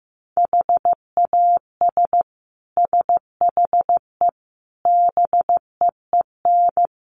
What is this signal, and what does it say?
Morse code, heard as a keyed tone, about 15 wpm, 700 Hz. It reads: HAS SHE BEEN
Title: Sentences from Top 700 Words 15wpm